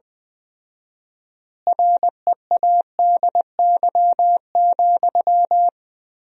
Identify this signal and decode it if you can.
Morse code, heard as a keyed tone, about 20 wpm, 700 Hz. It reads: READY,
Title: 4x Word-Spacing - Sentences from Top 500 Words - 20wpm